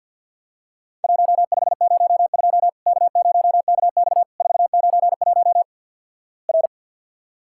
Morse code, heard as a keyed tone, about 50 wpm, 700 Hz. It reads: 1502 K0KX 491 R
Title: Worked All Europe DX Contest - QTC Exchange with Repeats - 50wpm